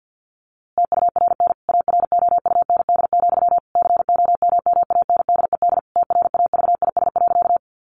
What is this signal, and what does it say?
Morse code, heard as a keyed tone, about 50 wpm, 700 Hz. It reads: TURN AROUND, COMMANDED TRAVIS.
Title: A Sound Of Thunder 50wpm